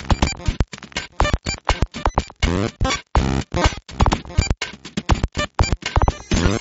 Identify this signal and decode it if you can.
There is no code.